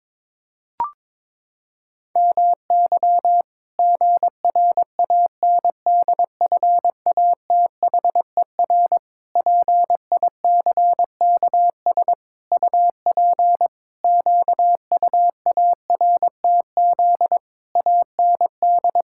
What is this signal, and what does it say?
Morse code, heard as a keyed tone, about 22 wpm, 700 Hz. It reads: MY GRANDFATHER PICKS UP QUARTZ AND
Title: Pangram Challenge v2 - 22wpm